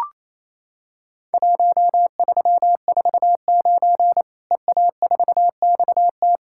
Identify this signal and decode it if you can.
Morse code, heard as a keyed tone, about 27 wpm, 700 Hz. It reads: 1349 EA4XT
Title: Worked All Europe DX Contest - QTC Exchange - 28wpm